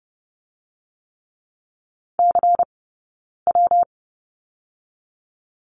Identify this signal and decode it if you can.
Morse code, heard as a keyed tone, about 30 wpm, 700 Hz. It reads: C W
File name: Lesson 20d - Number 3 - Two Characters 30wpm